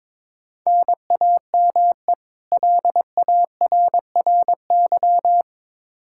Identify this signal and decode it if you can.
Morse code, heard as a keyed tone, about 22 wpm, 700 Hz. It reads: NAME LARRY
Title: Practice QSO - Exchange 1 - Beginning - 22wpm